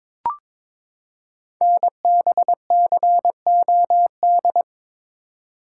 Morse code, heard as a keyed tone, about 23 wpm, 700 Hz. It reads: NBCOD
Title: Code Groups - 5 Letters - 22wpm